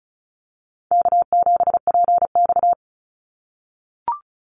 Morse code, heard as a keyed tone, about 34 wpm, 700 Hz. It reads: K7PX
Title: NA SOTA Activator Callsigns 35wpm